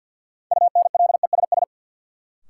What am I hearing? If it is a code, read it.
Morse code, heard as a keyed tone, about 52 wpm, 700 Hz. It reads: UNLESS
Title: Sentences from Top 1000 Words - Review Part 5 - New Word Drill - 50wpm